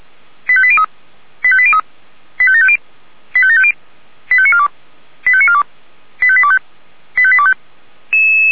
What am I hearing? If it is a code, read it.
Five-tone selective call (ZVEI), heard as quick runs of five tones, tones of about 70 ms. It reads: 85803 85803 85750 85750 85942 85942 85825 85825
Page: SDR/ZVEI